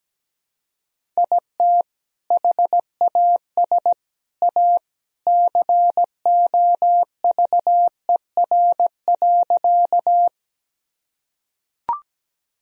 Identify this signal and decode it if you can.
Morse code, heard as a keyed tone, about 17 wpm, 700 Hz. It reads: IT HAS A COVER.
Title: Sentences from Top 1000 Words - Review Part 1 - 17wpm